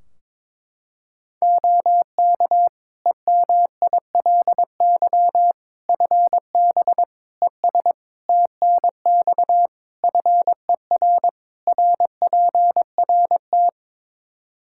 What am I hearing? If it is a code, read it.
Morse code, heard as a keyed tone, about 22 wpm, 700 Hz. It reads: OK EMILY FB ES TNX FER RPRT
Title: Practice QSO - Exchange 2 - Info - 22wpm